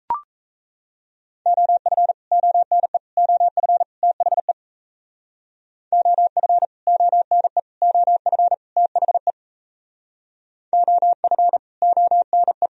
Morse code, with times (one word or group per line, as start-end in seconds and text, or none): 1.5-2.1 OF
2.3-3.0 ONE
3.2-3.8 OF
4.0-4.5 THE
5.9-6.7 OF
6.9-7.6 ONE
7.8-8.5 OF
8.8-9.3 THE
10.7-11.6 OF
11.8-12.7 ONE